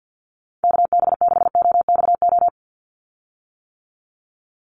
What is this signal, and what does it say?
Morse code, heard as a keyed tone, about 50 wpm, 700 Hz. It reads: KB6OXO
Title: USA Call Signs 50wpm